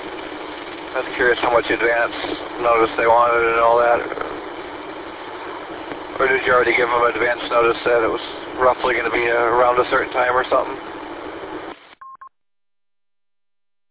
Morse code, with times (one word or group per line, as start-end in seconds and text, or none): none